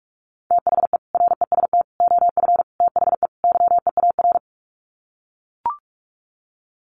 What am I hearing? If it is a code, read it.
Morse code, heard as a keyed tone, about 44 wpm, 700 Hz. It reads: THE REST OF THE YEAR
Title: Sets of 5 Words Encore 45wpm